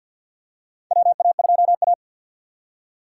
Morse code, heard as a keyed tone, about 52 wpm, 700 Hz. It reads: WA2A
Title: USA Call Signs - 4 Characters - 6x Character Spacing - 50wpm